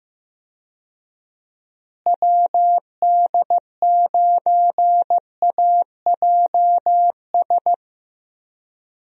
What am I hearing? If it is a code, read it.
Morse code, heard as a keyed tone, about 15 wpm, 700 Hz. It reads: WD9AJS
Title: USA Call Signs - 6 Characters - Repeated 3x - 15wpm